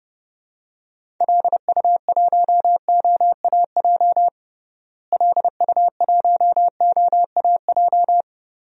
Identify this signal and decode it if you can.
Morse code, heard as a keyed tone, about 32 wpm, 700 Hz. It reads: LU1OAJ LU1OAJ
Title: DXCC-100 Call Signs - 6 Characters - Repeated 2x - 30wpm